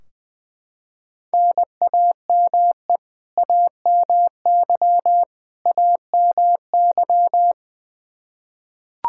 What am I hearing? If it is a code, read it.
Morse code, heard as a keyed tone, about 20 wpm, 700 Hz. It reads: NAME AMY AMY
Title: Practice QSO - Exchange 1 - Beginning - 20wpm